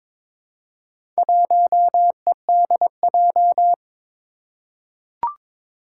Morse code, heard as a keyed tone, about 22 wpm, 700 Hz. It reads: 1EDJ
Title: Four Challenging Letter-Number Combinations 22wpm